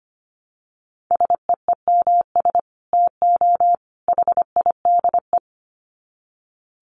Morse code, heard as a keyed tone, about 25 wpm, 700 Hz. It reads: SEEMS TO HIDE
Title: Dit Challenge 25wpm